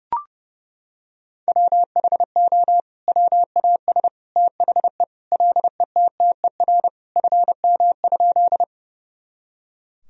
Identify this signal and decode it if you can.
Morse code, heard as a keyed tone, about 32 wpm, 700 Hz. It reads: WHO WAS THE LETTER FM?
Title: Sentences from Top 300 Words and Abbreviations 30wpm